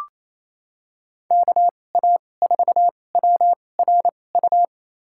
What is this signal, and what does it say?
Morse code, heard as a keyed tone, about 28 wpm, spaced as for 20 wpm, 700 Hz. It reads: KA4WRU
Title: USA Call Signs - 6 Characters - 2x Character Spacing - 28wpm